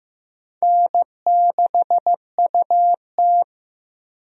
Morse code, heard as a keyed tone, about 15 wpm, 700 Hz. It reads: N6UT
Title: USA Call Signs - 4 Characters - Repeated 2x - 15wpm